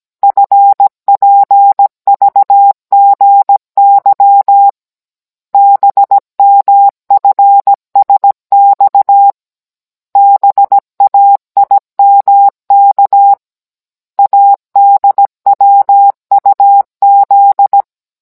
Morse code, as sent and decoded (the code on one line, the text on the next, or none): ..-. .--. ...- --. -.-- / -... -- ..-. ... -..- / -... .- .. -- -.- / .- -.. .-- ..- --..
FPVGY BMFSX BAIMK ADWUZ